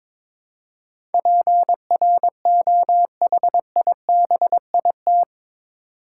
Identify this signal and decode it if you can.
Morse code, heard as a keyed tone, about 22 wpm, 700 Hz. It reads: PROHIBIT
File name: practice / 22wpm / Eight Letter Words - 22wpm